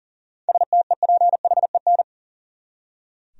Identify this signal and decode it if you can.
Morse code, heard as a keyed tone, about 40 wpm, 700 Hz. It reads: STEPHEN
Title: USA Common Names - 40wpm